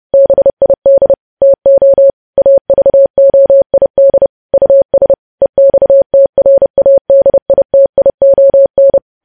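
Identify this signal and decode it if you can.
Morse code, heard as a keyed tone, about 30 wpm, 550 Hz. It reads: BID TO AVOID US EXTRADITION
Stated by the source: News headlines sent in Morse at 30.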